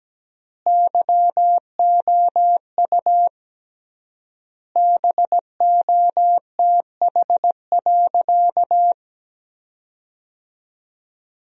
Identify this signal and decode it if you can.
Morse code, heard as a keyed tone, about 17 wpm, 700 Hz. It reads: YOU BOTH.
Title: 3x Word-Spacing - Sentences from Top 300 Words - 17wpm